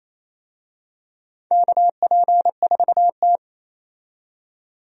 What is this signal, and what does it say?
Morse code, heard as a keyed tone, about 28 wpm, 700 Hz. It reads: KP4T
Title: DXCC-100 Call Signs - 4 Characters - 28wpm